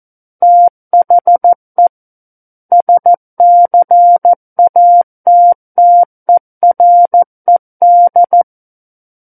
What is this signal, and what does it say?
Morse code, heard as a keyed tone, about 14 wpm, 700 Hz. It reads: THE SCATTERED
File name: Book One / 05-Chapter 5 - 14 WPM